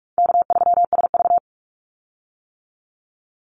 Morse code, heard as a keyed tone, about 46 wpm, 700 Hz. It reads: K3SV
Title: Easy Call Signs - 45wpm